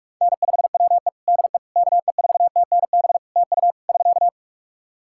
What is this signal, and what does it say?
Morse code, heard as a keyed tone, about 46 wpm, 700 Hz. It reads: N5WE DE KE4TNB TU <SK>
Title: K1USN SST QSO Practice 45wpm